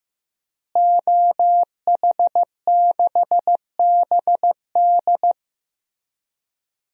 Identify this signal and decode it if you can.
Morse code, heard as a keyed tone, about 15 wpm, 700 Hz. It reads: OH6BD
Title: DXCC-100 Call Signs - 5 Characters - Repeated 3x - 15wpm